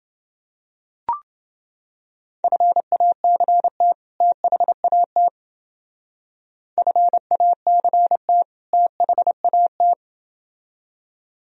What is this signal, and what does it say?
Morse code, heard as a keyed tone, about 29 wpm, 700 Hz. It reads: FACT THAT FACT THAT